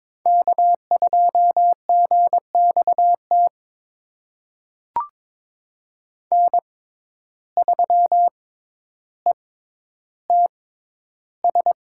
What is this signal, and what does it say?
Morse code, heard as a keyed tone, about 22 wpm, spaced as for 11 wpm, 700 Hz. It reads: K2GXT N3ETS
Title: USA Call Signs - 5 Characters - 6x Character Spacing - 22wpm